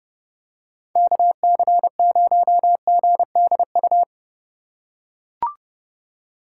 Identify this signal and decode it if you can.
Morse code, heard as a keyed tone, about 30 wpm, 700 Hz. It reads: KC0GDU